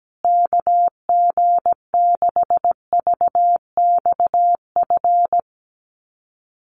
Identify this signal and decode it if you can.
Morse code, heard as a keyed tone, about 17 wpm, 700 Hz. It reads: KG6VXF
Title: USA Call Signs Repeated 2x 17wpm